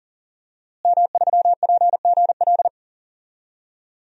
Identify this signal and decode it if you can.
Morse code, heard as a keyed tone, about 39 wpm, 700 Hz. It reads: M3PGL
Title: DXCC-100 Call Signs - 5 Characters - Repeated 2x - 40wpm